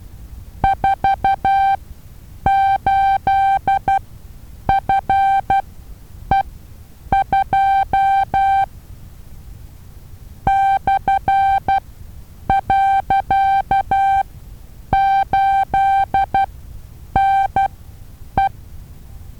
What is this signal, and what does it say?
Morse code, heard as a keyed tone, about 12 wpm, spaced as for 8 wpm, 780 Hz. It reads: 48FE2 /.8NE
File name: Taviro HA6VA B